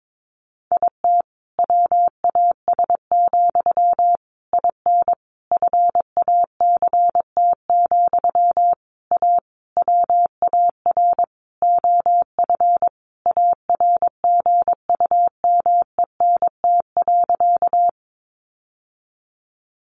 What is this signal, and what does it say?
Morse code, heard as a keyed tone, about 22 wpm, 700 Hz. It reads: IT WAS, IN FACT, A WAR OF ARGUMENT.
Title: Sentences from Top 1400 Words 22wpm